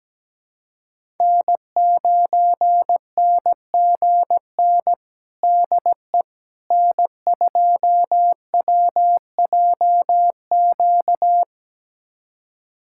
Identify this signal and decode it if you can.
Morse code, heard as a keyed tone, about 17 wpm, 700 Hz. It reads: N9NGN DE N2WJQ
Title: USA Callsign DE Callsign 17wpm